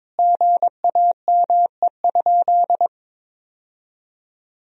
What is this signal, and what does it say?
Morse code, heard as a keyed tone, about 22 wpm, 700 Hz. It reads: GAME?
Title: Sentences from Top 1000 Words - Review Part 1 - 22wpm